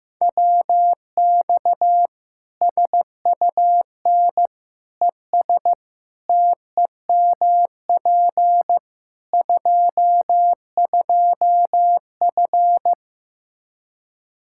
Morse code, heard as a keyed tone, about 15 wpm, 700 Hz. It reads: WX SUN ES TEMP 22F
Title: Practice QSO - Exchange 2 - Info - 15wpm